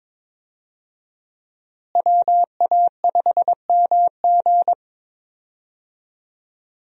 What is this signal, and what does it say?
Morse code, heard as a keyed tone, about 22 wpm, 700 Hz. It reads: WA5MG
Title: USA Call Signs - 5 Characters - Repeated 2x - 22wpm